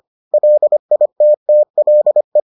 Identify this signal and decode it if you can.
Morse code, heard as a keyed tone, about 25 wpm, 600 Hz. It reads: LITTLE